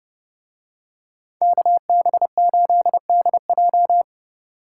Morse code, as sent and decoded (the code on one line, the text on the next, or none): -.- -... ---.. -.. .---
KB8DJ